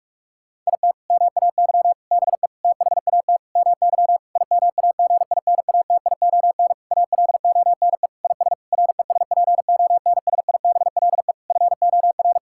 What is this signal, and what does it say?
Morse code, heard as a keyed tone, about 41 wpm, 700 Hz. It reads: IT MAY BE THAT MY IMAGINATION ALONE IS RESPONSIBLE FOR
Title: Sentences from Top 1600 Words 45wpm